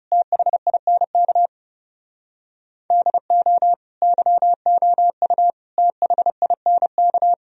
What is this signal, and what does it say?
Morse code, heard as a keyed tone, about 37 wpm, 700 Hz. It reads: THINK DO YOU THINK
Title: Speed-Racing - Sets of 3 Words 30wpm